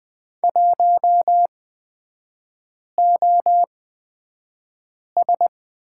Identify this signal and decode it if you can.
Morse code, heard as a keyed tone, about 20 wpm, spaced as for 5 wpm, 700 Hz. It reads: 1OS
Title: 243 Lesson 39e - Letter z - Three Characters 20wpm